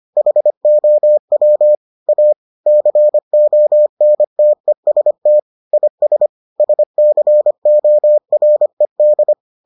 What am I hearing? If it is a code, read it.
Morse code, heard as a keyed tone, about 25 wpm, 600 Hz. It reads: HOW A CONTEST IS SCORED